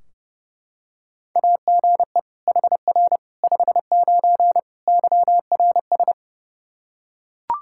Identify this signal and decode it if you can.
Morse code, heard as a keyed tone, about 30 wpm, 700 Hz. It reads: AGE HR 59 YRS
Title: Practice QSO - Exchange 3 - More Info - 30wpm